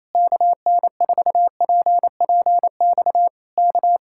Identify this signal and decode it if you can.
Morse code, heard as a keyed tone, about 27 wpm, 700 Hz. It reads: KN4PPX K
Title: Speed-Racing Common QSO Phrases - 28wpm